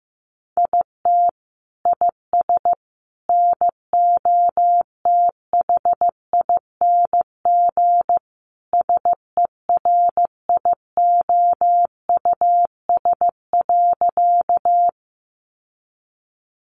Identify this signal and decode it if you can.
Morse code, heard as a keyed tone, about 15 wpm, 700 Hz. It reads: IT IS NOTHING SERIOUS.
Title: Sentences from Top 1100 Words 15wpm